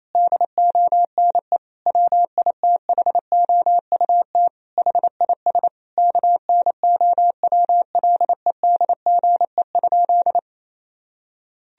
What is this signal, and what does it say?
Morse code, heard as a keyed tone, about 28 wpm, 700 Hz. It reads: DONE WITHOUT HIS KNOWLEDGE?
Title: Sentences from Top 1000 Words - Review Part 3 - 28wpm